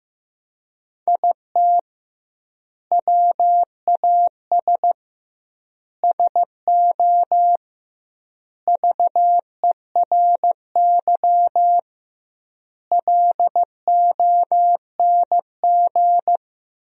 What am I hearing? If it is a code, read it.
Morse code, heard as a keyed tone, about 15 wpm, 700 Hz. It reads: IT WAS SO VERY LONG
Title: 2x Word-Spacing - Sentences from Top 500 Words - 15wpm